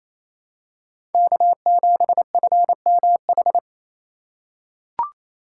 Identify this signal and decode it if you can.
Morse code, heard as a keyed tone, about 28 wpm, 700 Hz. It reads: K7FMH
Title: USA Call Signs - 5 Characters - 6x Character Spacing - 28wpm